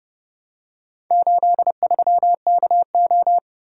Morse code, heard as a keyed tone, about 29 wpm, 700 Hz. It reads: 83KO